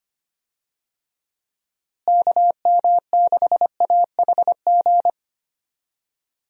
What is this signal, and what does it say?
Morse code, heard as a keyed tone, about 25 wpm, 700 Hz. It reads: KM6AHG
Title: USA Call Signs - 6 Characters - Repeated 2x - 25wpm